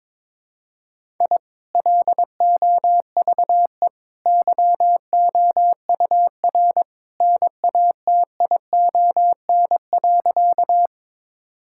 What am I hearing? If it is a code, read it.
Morse code, heard as a keyed tone, about 22 wpm, 700 Hz. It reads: I LOVE YOUR NATION.
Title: Sentences from Top 1000 Words - Review Part 6 - 22wpm